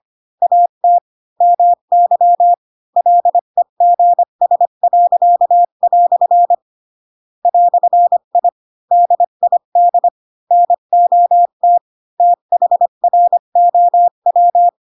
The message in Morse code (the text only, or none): AT MY LEGS." "I DID NOT THROW